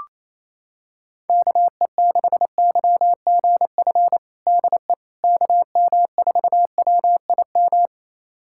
Morse code, heard as a keyed tone, about 28 wpm, 700 Hz. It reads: KE6YGF DE KM4WIM
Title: USA Callsign DE Callsign - 6 Characters - 28wpm